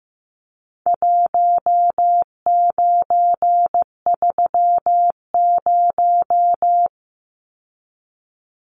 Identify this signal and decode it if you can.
Morse code, heard as a keyed tone, about 15 wpm, 700 Hz. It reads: 1930
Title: Numbers - Contest Style - 4-Digit 15wpm